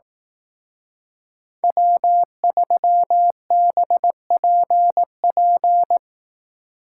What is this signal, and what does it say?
Morse code, heard as a keyed tone, about 18 wpm, 700 Hz. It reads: W3BPP
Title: Speed Racing - USA Call Signs 15wpm